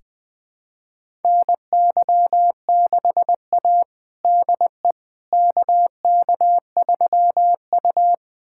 Morse code, heard as a keyed tone, about 20 wpm, 700 Hz. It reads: NY6A DE KK3U
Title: USA Callsign DE Callsign - 4 Characters - 20wpm